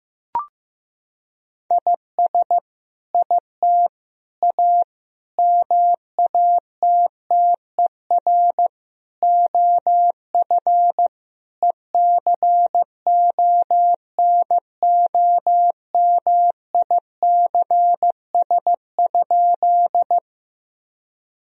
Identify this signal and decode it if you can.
Morse code, heard as a keyed tone, about 15 wpm, 700 Hz. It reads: IS IT A MATTER OF ECONOMICS?
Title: Sentences from Top 1000 Words - Review Part 3 - 15wpm